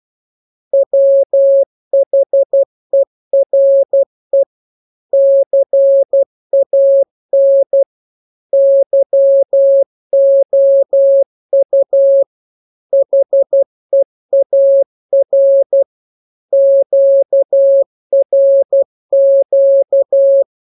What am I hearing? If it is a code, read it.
Morse code, heard as a keyed tone, about 12 wpm, 550 Hz. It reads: WHERE CAN YOU HEAR QRQ
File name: jokes / joke 10 12 0000